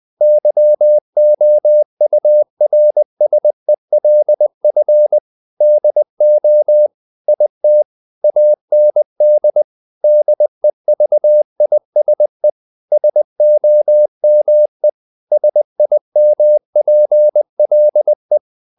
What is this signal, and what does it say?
Morse code, heard as a keyed tone, about 20 wpm, 600 Hz. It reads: YOURSELF DO IT AND DEVISE SOME SIMPLE
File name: the old man and the sea part 2 20wpm